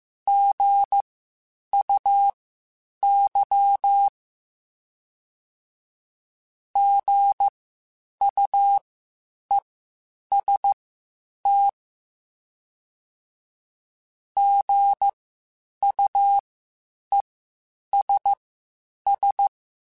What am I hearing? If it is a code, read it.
Morse code, heard as a keyed tone, about 15 wpm, spaced as for 8 wpm, 800 Hz. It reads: GUY GUEST GUESS
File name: Spaced Alphas / Spaced Alphas mp3s / Gs sp @13